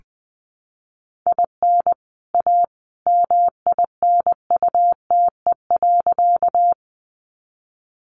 Morse code, heard as a keyed tone, about 20 wpm, 700 Hz. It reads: IN A MINUTE.
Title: Sentences from Top 500 Words 20wpm